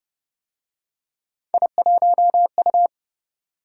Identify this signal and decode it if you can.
Morse code, heard as a keyed tone, about 27 wpm, 700 Hz. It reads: I1U